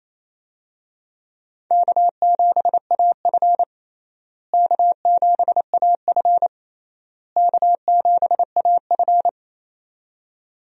Morse code, heard as a keyed tone, about 28 wpm, 700 Hz. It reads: K7AF K7AF K7AF
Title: USA Call Signs - 4 Characters - Repeated 3x - 28wpm